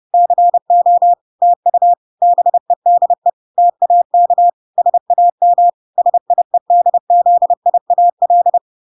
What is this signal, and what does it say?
Morse code, heard as a keyed tone, about 30 wpm, 700 Hz. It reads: CO TU BEDE TAK SAM SIEDZIAL